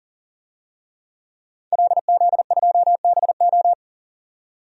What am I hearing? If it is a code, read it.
Morse code, heard as a keyed tone, about 39 wpm, 700 Hz. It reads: LZ2BO
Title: DXCC-100 Call Signs - 5 Characters - 40wpm